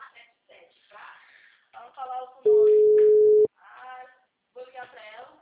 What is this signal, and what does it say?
Ringback tone (Europe, 425 Hz), with speech.
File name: out- -2015- - - .51126